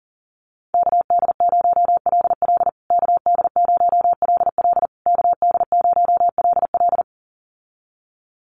Morse code, heard as a keyed tone, about 40 wpm, 700 Hz. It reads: KD0LL KD0LL KD0LL
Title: USA Call Signs Repeated 3x 40wpm